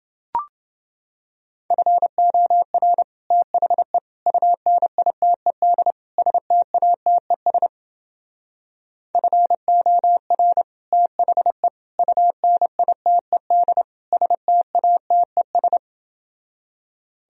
Morse code, heard as a keyed tone, about 28 wpm, 700 Hz. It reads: FOR THE UNITED STATES FOR THE UNITED STATES